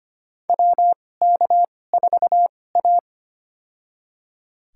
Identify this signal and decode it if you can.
Morse code, heard as a keyed tone, about 25 wpm, spaced as for 18 wpm, 700 Hz. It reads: WK4A
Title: USA Call Signs - 4 Characters - 2x Character Spacing - 25wpm